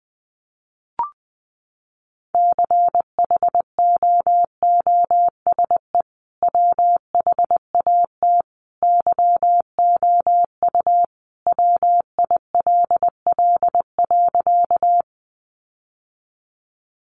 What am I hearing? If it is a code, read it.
Morse code, heard as a keyed tone, about 20 wpm, 700 Hz. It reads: CHOOSE WHAT YOU WILL.
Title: Sentences from Top 700 Words 20wpm